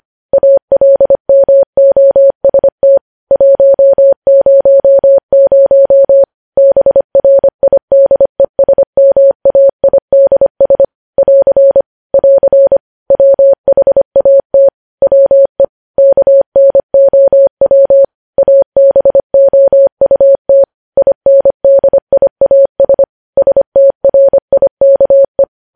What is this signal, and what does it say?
Morse code, heard as a keyed tone, about 25 wpm, 550 Hz. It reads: ALMOST 100 BRIDESMAIDS <AR> <AR> WHAT WE KNOW ABOUT INDIAS STRIKE
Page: news headlines in morse code at 25 wpm